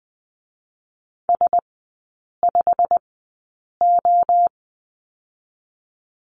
Morse code, heard as a keyed tone, about 20 wpm, spaced as for 8 wpm, 690 Hz. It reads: S5O
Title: 087 Lesson 15e - Number 5 - Three Characters 20wpm